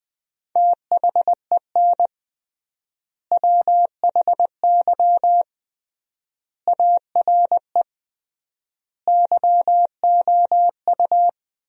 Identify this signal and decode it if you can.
Morse code, heard as a keyed tone, about 20 wpm, 700 Hz. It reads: THEN WHY ARE YOU
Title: 3x Word-Spacing - Sentences from Top 200 Words - 20wpm